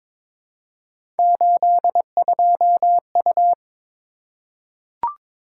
Morse code, heard as a keyed tone, about 22 wpm, 700 Hz. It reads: 82U
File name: practice / 22wpm / Three Letter-Number Combinations - 22wpm